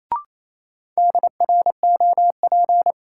Morse code, heard as a keyed tone, about 28 wpm, 700 Hz. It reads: DROP
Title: Sentences from Top 1400 Words - New Word Drill Alt - 28wpm